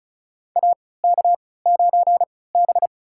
Morse code, heard as a keyed tone, about 34 wpm, spaced as for 20 wpm, 700 Hz. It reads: AK9B
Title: USA Call Signs - 4 Characters - 3x Character Spacing - 35wpm